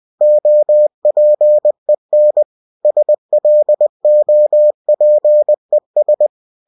Morse code, heard as a keyed tone, about 20 wpm, 600 Hz. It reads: OPEN SLOPES
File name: cw-books / winnie-the-pooh / winnie-20wpm-page-0028